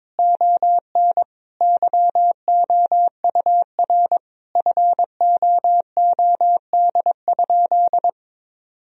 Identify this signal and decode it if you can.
Morse code, heard as a keyed tone, about 21 wpm, 700 Hz. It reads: ON YOUR FOOD?
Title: Sentences from Top 1000 Words - Review Part 2 - 22wpm